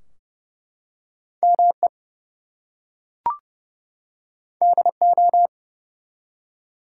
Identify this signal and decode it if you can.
Morse code, heard as a keyed tone, about 30 wpm, 700 Hz. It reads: ME DO
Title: Two Letter Words - 30wpm